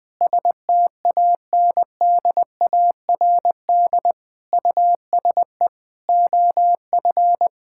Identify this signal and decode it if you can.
Morse code, heard as a keyed tone, about 20 wpm, 700 Hz. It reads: STANDARD USE OF
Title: Sentences from Top 1000 Words - Review Part 6 - 20wpm